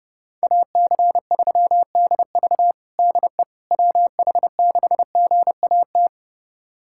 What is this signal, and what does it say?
Morse code, heard as a keyed tone, about 30 wpm, 700 Hz. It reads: AC3DV DE WH6GAT
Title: USA Callsign DE Callsign 30wpm